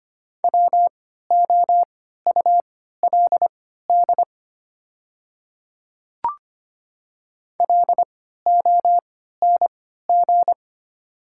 Morse code, heard as a keyed tone, about 25 wpm, spaced as for 14 wpm, 700 Hz. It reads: WOULD LONG
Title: 3x Character Spacing - Top 200 Words - 25wpm